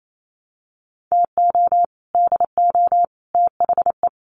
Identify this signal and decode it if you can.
Morse code, heard as a keyed tone, about 28 wpm, 700 Hz. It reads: TO DO THE
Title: Sets of 5 Words 28wpm